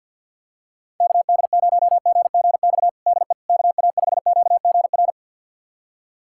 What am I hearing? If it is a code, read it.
Morse code, heard as a keyed tone, about 48 wpm, 700 Hz. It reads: KD0GGX DE KA5QGR
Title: USA Callsign DE Callsign 50wpm